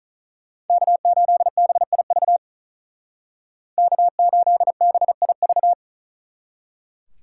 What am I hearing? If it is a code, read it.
Morse code, heard as a keyed tone, about 41 wpm, 700 Hz. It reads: K8BIV K8BIV
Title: Speed Racing - USA 1x3 Call Signs 35wpm